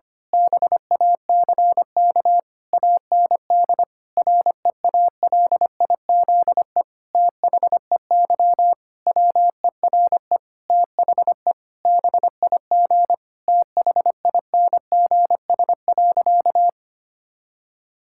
Morse code, heard as a keyed tone, about 25 wpm, 700 Hz. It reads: BACK AND REALIZE THEY WERE THE BIG THINGS.
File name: practice / 25wpm / Difficult Sentences with Hints - 25wpm